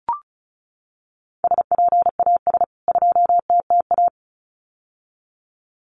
Morse code, heard as a keyed tone, about 34 wpm, 700 Hz. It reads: SPAS 2TTA
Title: CWT Exchange 35wpm